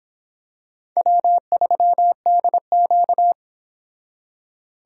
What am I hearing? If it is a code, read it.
Morse code, heard as a keyed tone, about 26 wpm, 700 Hz. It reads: W3DQ